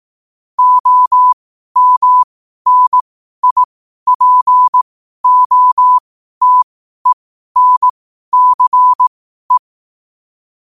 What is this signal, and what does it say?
Morse code, heard as a keyed tone, about 18 wpm, spaced as for 12 wpm, 1000 Hz. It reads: OMNIPOTENCE
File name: qotdfiles / 12wpm